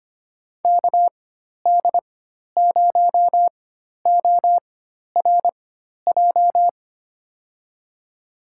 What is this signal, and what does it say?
Morse code, heard as a keyed tone, about 25 wpm, spaced as for 12 wpm, 700 Hz. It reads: KD0ORJ
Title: USA Call Signs - 6 Characters - 4x Character Spacing - 25wpm